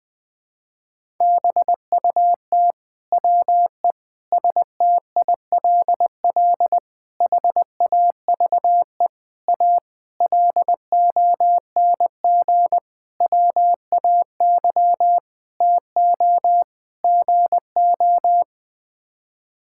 Morse code, heard as a keyed tone, about 20 wpm, 700 Hz. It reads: BUT WE STILL HAVE A LONG WAY TO GO
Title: Sets of 9 Words - 20wpm